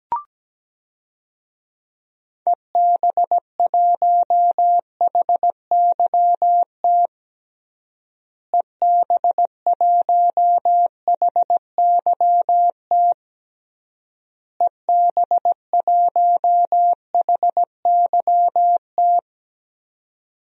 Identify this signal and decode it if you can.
Morse code, heard as a keyed tone, about 17 wpm, 700 Hz. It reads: EB1HYT EB1HYT EB1HYT
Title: DXCC-100 Call Signs - 6 Characters - Repeated 3x - 17wpm